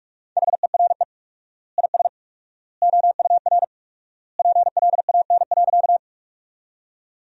Morse code, heard as a keyed tone, about 46 wpm, 700 Hz. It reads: HERE IS OUR PLAN.
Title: 4x Word-Spacing - Sentences from Top 500 Words - 45wpm